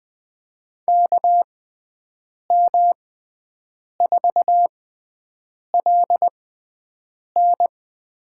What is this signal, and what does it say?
Morse code, heard as a keyed tone, about 20 wpm, spaced as for 7 wpm, 700 Hz. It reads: KM4LN
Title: USA Call Signs - 6 Characters - 6x Character Spacing - 20wpm